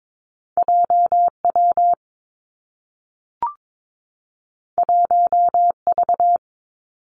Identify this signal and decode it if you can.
Morse code, heard as a keyed tone, about 22 wpm, 700 Hz. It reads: J W 1 V
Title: Two Challenging Letter-Number Combinations 22wpm